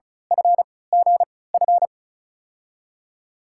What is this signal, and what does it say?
Morse code, heard as a keyed tone, about 36 wpm, spaced as for 20 wpm, 700 Hz. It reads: FGF
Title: USA Call Signs - 6 Characters - 3x Character Spacing - 35wpm